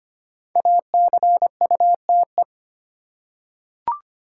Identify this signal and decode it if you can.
Morse code, heard as a keyed tone, about 25 wpm, 700 Hz. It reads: ACUTE
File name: 5 Letter Words - First 3 Letters Given - 25wpm